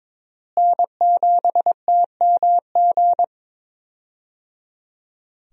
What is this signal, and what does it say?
Morse code, heard as a keyed tone, about 22 wpm, 700 Hz. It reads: N7TMG
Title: USA Call Signs - 5 Characters - Repeated 3x - 22wpm